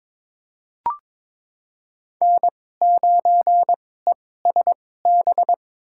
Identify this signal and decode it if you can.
Morse code, heard as a keyed tone, about 22 wpm, spaced as for 16 wpm, 700 Hz. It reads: N9ESB
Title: USA Call Signs - 5 Characters - 2x Character Spacing - 22wpm